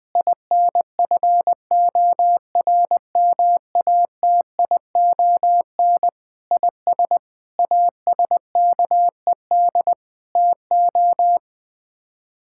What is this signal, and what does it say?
Morse code, heard as a keyed tone, about 21 wpm, 700 Hz. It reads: INFORMATION IS ASKED TO
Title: Sets of 6 Words - 20wpm